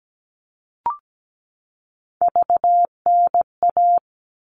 Morse code, heard as a keyed tone, about 17 wpm, 700 Hz. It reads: VNA
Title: All Cut Numbers - 3-Digit 17wpm